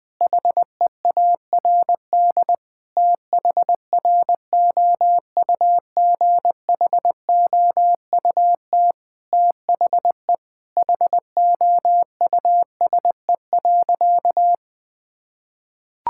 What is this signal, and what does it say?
Morse code, heard as a keyed tone, about 20 wpm, 700 Hz. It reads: HEARD THROUGHOUT THE HOUSE.
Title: Sentences from Top 1000 Words - Review Part 5 - 20wpm